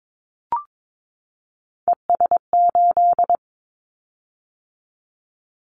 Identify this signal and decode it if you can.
Morse code, heard as a keyed tone, about 22 wpm, 700 Hz. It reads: ES8
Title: Callsign Prefixes for Top 50 DXCC - 22wpm